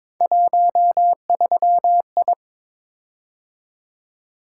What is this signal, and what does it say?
Morse code, heard as a keyed tone, about 22 wpm, 700 Hz. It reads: 13I
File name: practice / 22wpm / Three Letter-Number Combinations - 22wpm